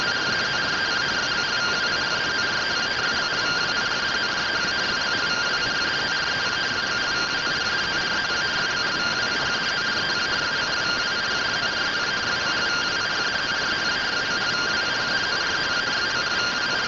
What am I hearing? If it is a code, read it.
Two-tone teleprinter signal, about 170 Hz shift.